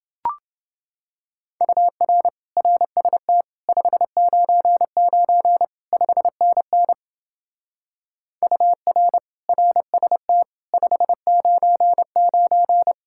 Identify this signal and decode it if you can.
Morse code, heard as a keyed tone, about 28 wpm, 700 Hz. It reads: UR RST 599 5NN UR RST 599